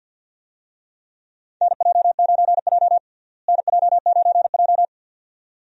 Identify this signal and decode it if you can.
Morse code, heard as a keyed tone, about 47 wpm, 700 Hz. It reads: NJ9J NJ9J
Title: USA Call Signs - 4 Characters - Repeated 2x - 50wpm